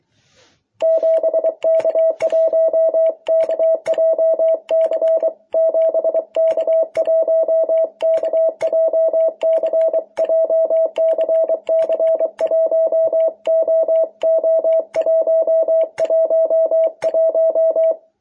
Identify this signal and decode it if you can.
Morse code, heard as a keyed tone, about 24 wpm, 660 Hz. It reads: 7K1KJC 7K1KJC JCC1OO111
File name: jh8cbh / 2021cw